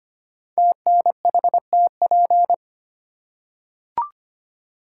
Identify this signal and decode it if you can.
Morse code, heard as a keyed tone, about 25 wpm, 700 Hz. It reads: TNHTP
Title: Code Groups - 5 Letters - 25wpm